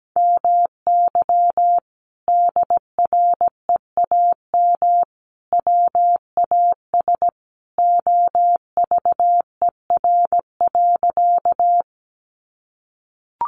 Morse code, heard as a keyed tone, about 17 wpm, 700 Hz. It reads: MY DREAM WAS OVER.
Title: Sentences from Top 1500 Words 17wpm